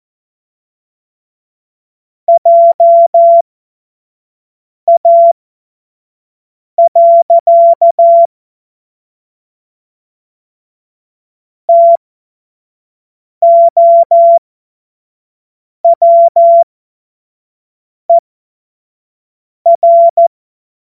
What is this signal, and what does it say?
Morse code, heard as a keyed tone, about 14 wpm, spaced as for 5 wpm, 680 Hz. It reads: JA. TOWER
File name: Morsekurs / Witze / Witz 30. Mai 2023 25 F70